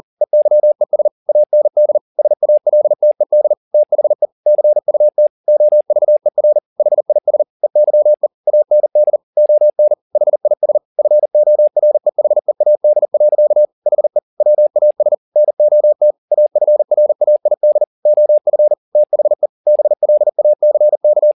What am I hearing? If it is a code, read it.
Morse code, heard as a keyed tone, about 40 wpm, 600 Hz. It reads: EYES AND SALTED THE CUT OVER HIS EYE AND ON HIS FOREHEAD. HE WAS NOT AFRAID OF THE BLACK
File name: the old man and the sea 2 40wpm